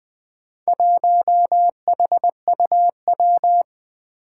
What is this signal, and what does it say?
Morse code, heard as a keyed tone, about 20 wpm, 700 Hz. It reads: 1HUW
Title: Four Challenging Letter-Number Combinations 20wpm